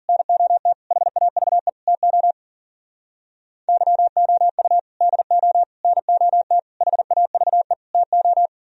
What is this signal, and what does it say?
Morse code, heard as a keyed tone, about 54 wpm, 700 Hz. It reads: NOT HAVE TO YOU DO NOT HAVE TO